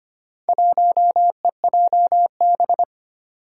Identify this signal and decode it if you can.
Morse code, heard as a keyed tone, about 25 wpm, 700 Hz. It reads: 1EJB
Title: Four Challenging Letter-Number Combinations 25wpm